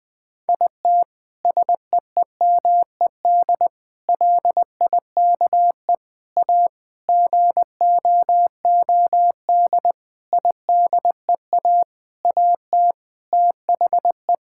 Morse code, heard as a keyed tone, about 20 wpm, 700 Hz. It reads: IT SEEMED LIKE A GOOD IDEA AT THE
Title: Sentences with 9-Word Phrases - Spoken Beginning - 20wpm